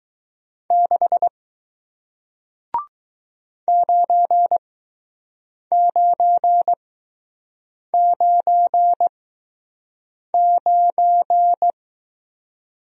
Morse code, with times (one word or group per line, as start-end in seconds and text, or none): 0.7-1.3 6
3.7-11.7 9999